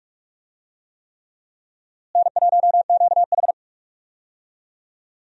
Morse code, heard as a keyed tone, about 42 wpm, 700 Hz. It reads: N1QH
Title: USA Call Signs - 4 Characters - Repeated 3x - 45wpm